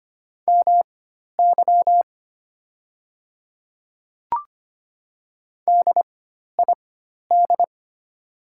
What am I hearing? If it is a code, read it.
Morse code, heard as a keyed tone, about 25 wpm, spaced as for 12 wpm, 690 Hz. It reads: MY DID
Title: 4x Character Spacing - Top 100 Words - 25wpm